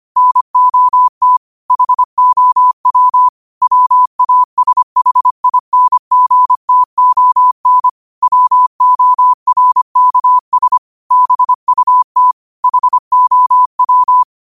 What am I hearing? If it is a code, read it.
Morse code, heard as a keyed tone, about 25 wpm, 1000 Hz. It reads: NOT HOW WASHINGTON WORKS BUT HOW